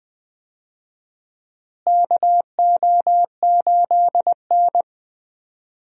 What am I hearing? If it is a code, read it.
Morse code, heard as a keyed tone, about 21 wpm, 700 Hz. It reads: KO8N
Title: USA Call Signs - 4 Characters - Repeated 3x - 20wpm